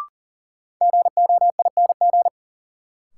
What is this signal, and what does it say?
Morse code, heard as a keyed tone, about 38 wpm, 700 Hz. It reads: GOING
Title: Sets of 2 Words - New Word Drill - 40wpm